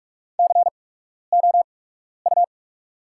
Morse code, heard as a keyed tone, about 46 wpm, spaced as for 12 wpm, 700 Hz. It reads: COU
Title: 8x Character Spacing - Top 500 Words - 45wpm